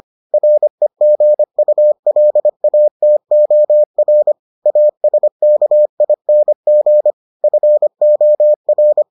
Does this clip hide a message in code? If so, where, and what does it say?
Yes: Morse code from 0.3 to 9.0 s, heard as a keyed tone, about 25 wpm, 600 Hz. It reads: REGULATOR ASKING FOR